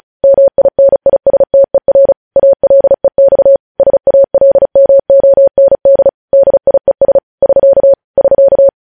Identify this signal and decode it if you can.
Morse code, heard as a keyed tone, about 36 wpm, 550 Hz. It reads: MINISTER ALEX SALMOND DIES <SK> <SK>